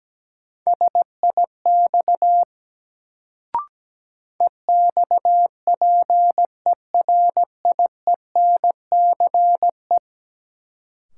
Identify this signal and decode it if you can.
Morse code, heard as a keyed tone, about 17 wpm, 700 Hz. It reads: SIX EXPERIENCE
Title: Sentences from Top 600 Words - New Word Drill - 17wpm